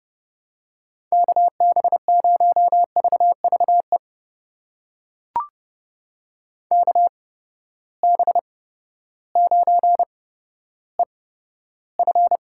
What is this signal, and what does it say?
Morse code, heard as a keyed tone, about 30 wpm, spaced as for 14 wpm, 700 Hz. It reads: KB0VVE KB9EF
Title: USA Call Signs - 8x Character Spacing - 30wpm